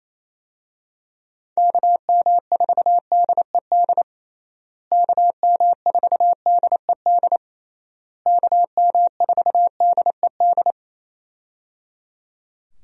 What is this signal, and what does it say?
Morse code, heard as a keyed tone, about 28 wpm, 700 Hz. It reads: KM4DED KM4DED KM4DED
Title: USA Call Signs - 6 Characters - Repeated 3x - 28wpm